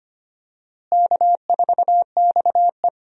Morse code, heard as a keyed tone, about 25 wpm, 700 Hz. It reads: K4XE
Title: USA Call Signs - 4 Characters - Repeated 2x - 25wpm